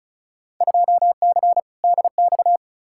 Morse code, heard as a keyed tone, about 36 wpm, 700 Hz. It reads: 2C DX
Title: Field Day Exchanges Encore - 35wpm